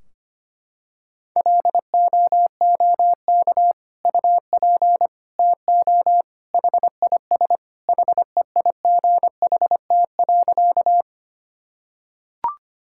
Morse code, heard as a keyed tone, about 25 wpm, 700 Hz. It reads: LOOK UP TO HIS HEIGHT.
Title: Sentences from Top 1600 Words 25wpm